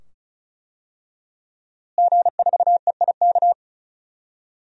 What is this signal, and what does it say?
Morse code, heard as a keyed tone, about 34 wpm, 700 Hz. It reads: G4EIK
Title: DXCC-100 Call Signs - 5 Characters - Repeated 2x - 35wpm